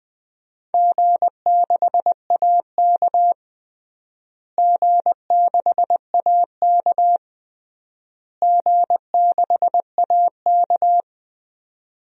Morse code, heard as a keyed tone, about 20 wpm, 700 Hz. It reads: G6AK G6AK G6AK
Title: DXCC-100 Call Signs - 4 Characters - Repeated 3x - 20wpm